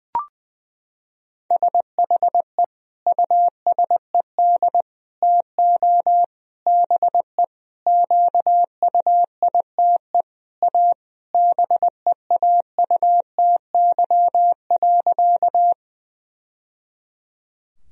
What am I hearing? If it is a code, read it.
Morse code, heard as a keyed tone, about 20 wpm, 700 Hz. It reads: SHE USED TO BE QUITE A BEAUTY.
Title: Sentences from Top 1000 Words - Review Part 2 - 20wpm